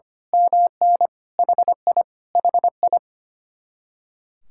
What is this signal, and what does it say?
Morse code, heard as a keyed tone, about 24 wpm, 700 Hz. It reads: MN HI HI
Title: Speed-Racing Common QSO Phrases - 25wpm